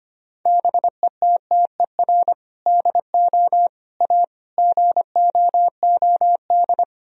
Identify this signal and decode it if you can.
Morse code, heard as a keyed tone, about 25 wpm, 700 Hz. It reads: BETTER DO A GOOD
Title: Sentences from Top 1000 Words - Review Part 3 - 25wpm